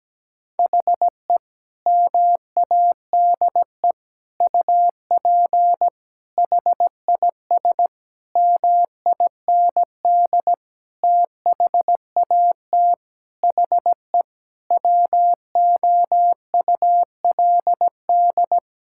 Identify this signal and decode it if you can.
Morse code, heard as a keyed tone, about 17 wpm, 700 Hz. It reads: HE MADE UP HIS MIND THAT HE WOULD
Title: Sentences from Top 1600 Words 17wpm